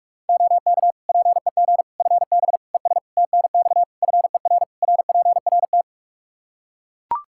Morse code, heard as a keyed tone, about 42 wpm, 700 Hz. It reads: OK PEG FB ES TNX FER RPRT
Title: Practice QSO - Exchange 2 - Info - 45wpm